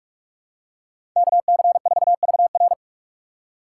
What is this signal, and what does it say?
Morse code, heard as a keyed tone, about 46 wpm, 700 Hz. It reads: KC4VR
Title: USA Call Signs - 5 Characters - 2x Character Spacing - 45wpm